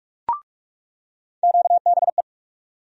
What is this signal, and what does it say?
Morse code, heard as a keyed tone, about 46 wpm, 700 Hz. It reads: QBE
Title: USA Call Sign Postfixes - 3 Letters 45wpm